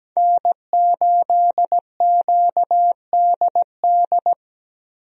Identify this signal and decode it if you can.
Morse code, heard as a keyed tone, about 17 wpm, 700 Hz. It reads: N8QDD
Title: USA Call Signs - 5 Characters - 17wpm